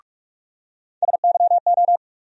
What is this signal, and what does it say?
Morse code, heard as a keyed tone, about 43 wpm, 700 Hz. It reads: SYO
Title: USA Call Sign Postfixes - 3 Letters 45wpm